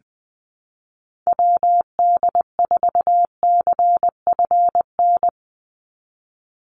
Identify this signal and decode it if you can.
Morse code, heard as a keyed tone, about 20 wpm, 700 Hz. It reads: WD4CFN
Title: NA SOTA Chaser Callsigns 20wpm